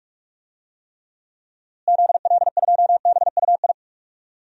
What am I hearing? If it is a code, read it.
Morse code, heard as a keyed tone, about 41 wpm, 700 Hz. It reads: ZL2BUI
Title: DXCC-100 Call Signs - 6 Characters - Repeated 2x - 45wpm